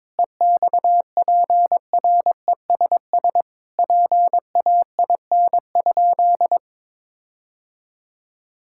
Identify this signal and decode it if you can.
Morse code, heard as a keyed tone, about 22 wpm, 700 Hz. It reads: EXPRESS PAIN?
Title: Sentences from Top 1600 Words 22wpm